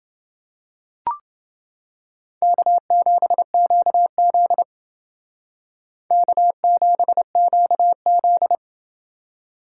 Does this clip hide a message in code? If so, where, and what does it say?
Yes: Morse code from 2.4 to 8.6 s, heard as a keyed tone, about 28 wpm, 700 Hz. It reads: K7QZ K7QZ